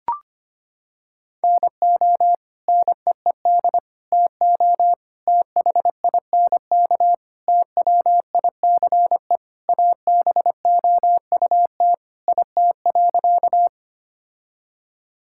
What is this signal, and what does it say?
Morse code, heard as a keyed tone, about 25 wpm, 700 Hz. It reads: NO NEED TO THINK TWICE ABOUT IT.
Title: Sentences from Top 1600 Words 25wpm